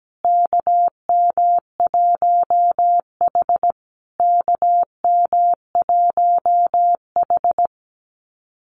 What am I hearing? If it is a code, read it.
Morse code, heard as a keyed tone, about 17 wpm, 700 Hz. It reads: KM1H KM1H
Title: USA Call Signs Repeated 3x 17wpm